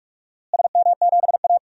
Morse code, heard as a keyed tone, about 45 wpm, 700 Hz. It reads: SM7A
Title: DXCC-100 Call Signs - 4 Characters - Repeated 3x - 45wpm